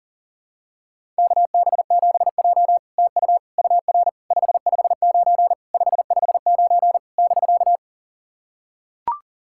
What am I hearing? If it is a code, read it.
Morse code, heard as a keyed tone, about 40 wpm, 700 Hz. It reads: KB7J TU UR 559 559 <BK>
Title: SOTA QSO Practice 40wpm